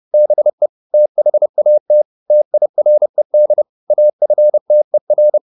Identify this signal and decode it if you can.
Morse code, heard as a keyed tone, about 30 wpm, 600 Hz. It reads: BE THAT TIRED AFTER
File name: the old man and the sea / the old man and the sea 1 30wpm